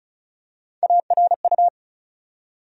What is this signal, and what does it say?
Morse code, heard as a keyed tone, about 34 wpm, 700 Hz. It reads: ARU
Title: USA Call Sign Postfixes - 3 Letters 35wpm